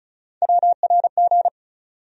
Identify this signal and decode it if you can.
Morse code, heard as a keyed tone, about 36 wpm, 700 Hz. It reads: WRG
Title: USA Call Sign Postfixes - 3 Letters 35wpm